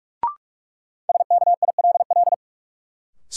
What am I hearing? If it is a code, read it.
Morse code, heard as a keyed tone, about 44 wpm, 700 Hz. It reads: SKILL
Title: Sentences from Top 1200 Words - New Word Drill Alt - 45wpm